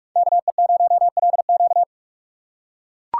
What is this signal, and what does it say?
Morse code, heard as a keyed tone, about 45 wpm, 690 Hz. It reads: KE0LQ
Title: USA Call Signs - 4x Character Spacing - 45wpm